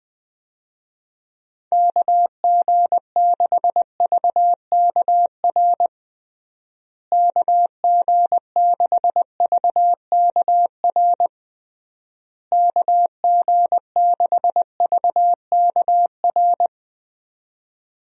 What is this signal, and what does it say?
Morse code, heard as a keyed tone, about 20 wpm, 700 Hz. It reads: KG6VKR KG6VKR KG6VKR
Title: USA Call Signs - 6 Characters - Repeated 3x - 20wpm